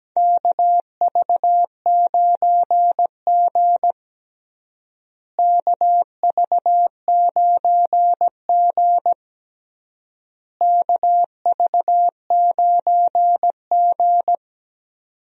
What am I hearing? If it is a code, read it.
Morse code, heard as a keyed tone, about 17 wpm, 700 Hz. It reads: KV9G KV9G KV9G
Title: USA Call Signs - 4 Characters - Repeated 3x - 17wpm